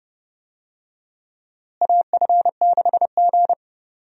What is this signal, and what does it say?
Morse code, heard as a keyed tone, about 30 wpm, 700 Hz. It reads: AF6G
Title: USA Call Signs - 4 Characters - Repeated 3x - 30wpm